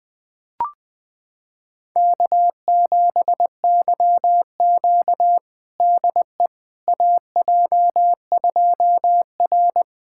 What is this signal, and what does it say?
Morse code, heard as a keyed tone, about 20 wpm, 700 Hz. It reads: K7YQ DE AJ2R
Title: USA Callsign DE Callsign - 4 Characters - 20wpm